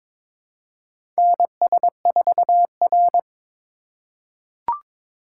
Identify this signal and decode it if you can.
Morse code, heard as a keyed tone, about 22 wpm, 700 Hz. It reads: NS4R
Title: USA Call Signs - 4 Characters - 4x Character Spacing - 22wpm